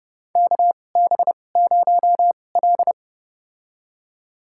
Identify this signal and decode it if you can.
Morse code, heard as a keyed tone, about 30 wpm, spaced as for 22 wpm, 700 Hz. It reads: KB0L
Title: USA Call Signs - 4 Characters - 2x Character Spacing - 30wpm